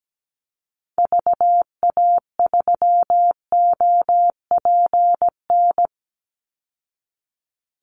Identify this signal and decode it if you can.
Morse code, heard as a keyed tone, about 17 wpm, 700 Hz. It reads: VA3OPN
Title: Call Signs Repeated 3x - 17wpm